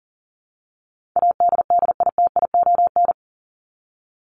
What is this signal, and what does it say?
Morse code, heard as a keyed tone, about 40 wpm, 700 Hz. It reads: ADDITION
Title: Top 751-1000 Words 40wpm